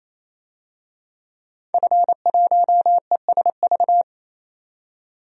Morse code, heard as a keyed tone, about 28 wpm, 700 Hz. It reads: F1ESV
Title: DXCC-100 Call Signs - 5 Characters - 28wpm